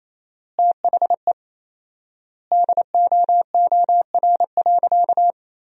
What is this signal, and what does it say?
Morse code, heard as a keyed tone, about 28 wpm, 700 Hz. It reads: THE DOOR.
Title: 4x Word-Spacing - Sentences from Top 500 Words - 28wpm